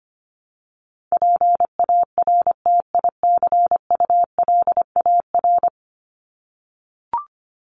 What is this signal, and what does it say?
Morse code, heard as a keyed tone, about 25 wpm, 700 Hz. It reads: PARTICULAR
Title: LY Suffix - Top 100 Words with Base Words 25wpm